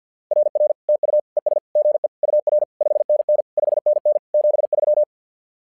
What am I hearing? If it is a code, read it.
Morse code, heard as a keyed tone, about 50 wpm, 600 Hz. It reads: RR TU ES GE UR 5NN 5NN 73
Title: SOTA QSO Practice 50wpm